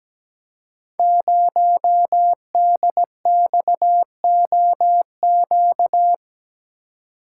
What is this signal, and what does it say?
Morse code, heard as a keyed tone, about 17 wpm, 700 Hz. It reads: 0DXOQ
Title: Code Groups - 5 Letter:Numbers - 17wpm